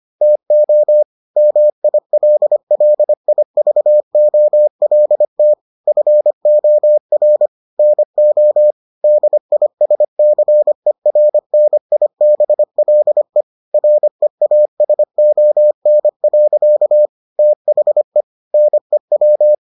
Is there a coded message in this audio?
Yes: Morse code from 0.2 to 19.5 s, heard as a keyed tone, about 25 wpm, 600 Hz. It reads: TO MILLIVOLT FOR NO DISCERNIBLE REASON. THE NEW